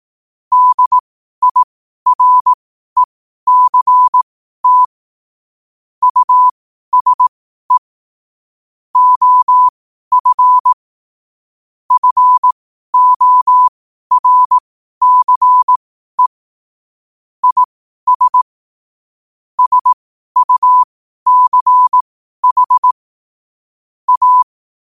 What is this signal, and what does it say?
Morse code, heard as a keyed tone, about 18 wpm, spaced as for 12 wpm, 1000 Hz. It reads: DIRECT USE OF FORCE IS SUCH A
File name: qotdfiles / 12wpm